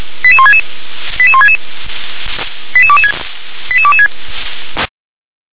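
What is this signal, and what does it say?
Five-tone selective call (ZVEI), heard as quick runs of five tones, tones of about 70 ms. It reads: 88166 88166 88226 88226
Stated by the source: ZVEI five-tone selective call received on SDR.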